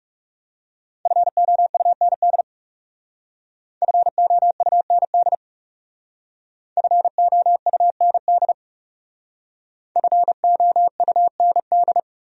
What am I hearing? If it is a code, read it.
Morse code, heard as a keyed tone, about 38 wpm, 700 Hz. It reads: FOUND FOUND FOUND FOUND